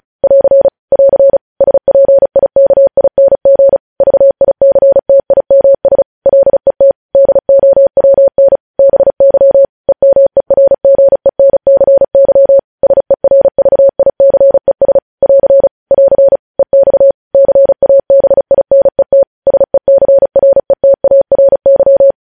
Morse code, heard as a keyed tone, about 36 wpm, 550 Hz. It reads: <AR> <AR> SPIKING VICTIMS LET DOWN BY EMERGENCY SERVICES <AR> <AR> EX CABINET SECRETARY